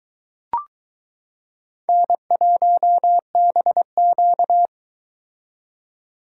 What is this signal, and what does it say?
Morse code, heard as a keyed tone, about 22 wpm, 700 Hz. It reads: N1BQ